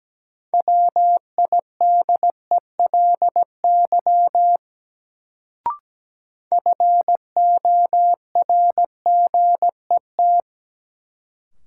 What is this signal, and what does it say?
Morse code, heard as a keyed tone, about 17 wpm, 700 Hz. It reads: WIDELY FORGET
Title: Sentences from Top 1200 Words - New Word Drill - 17wpm